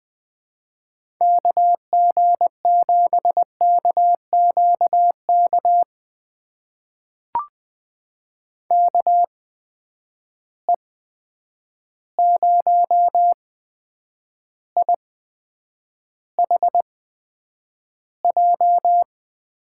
Morse code, heard as a keyed tone, about 20 wpm, spaced as for 9 wpm, 700 Hz. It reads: KG7KQK KE0IHJ
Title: USA Call Signs - 6 Characters - 8x Character Spacing - 20wpm